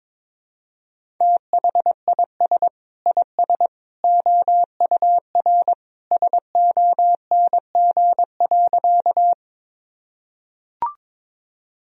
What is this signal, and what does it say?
Morse code, heard as a keyed tone, about 21 wpm, 700 Hz. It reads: THIS IS OUR SONG.
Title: Sentences from Top 1000 Words - Review Part 2 - 22wpm